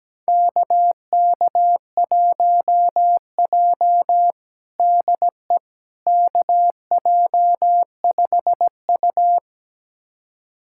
Morse code, heard as a keyed tone, about 17 wpm, 700 Hz. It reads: KK1J DE KJ5U
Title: USA Callsign DE Callsign - 4 Characters - 17wpm